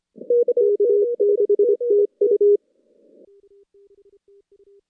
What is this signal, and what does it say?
Morse code, heard as a keyed tone, about 26 wpm, 400 Hz. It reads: K6TU
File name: contests / CW-SS-k5tr- -2100